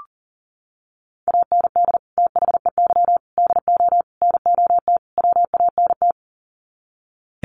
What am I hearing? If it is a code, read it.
Morse code, heard as a keyed tone, about 40 wpm, 700 Hz. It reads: AND THEY DO NOT WANT
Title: Sets of 5 Words Encore2 40wpm